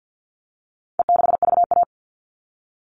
Morse code, heard as a keyed tone, about 50 wpm, 700 Hz. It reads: E6VA
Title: All Cut Numbers - 4-Digit 50wpm